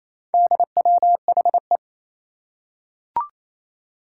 Morse code, heard as a keyed tone, about 28 wpm, 700 Hz. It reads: DWHE
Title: Four Challenging Letter-Number Combinations 28wpm